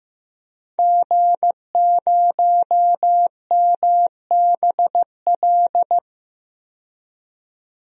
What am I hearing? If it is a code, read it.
Morse code, heard as a keyed tone, about 15 wpm, 700 Hz. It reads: G0MBL
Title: DXCC-100 Call Signs - 5 Characters - 15wpm